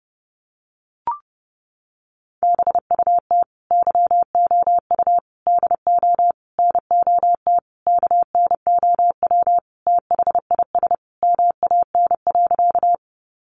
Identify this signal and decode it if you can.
Morse code, heard as a keyed tone, about 30 wpm, 700 Hz. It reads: BUT YOU DO NOT KNOW THIS MAN.
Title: Sentences from Top 600 Words 30wpm